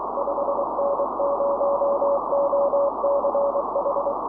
Morse code, heard as a keyed tone, about 23 wpm, 560 Hz, with noise. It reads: 5N0OCH
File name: 5N0OCH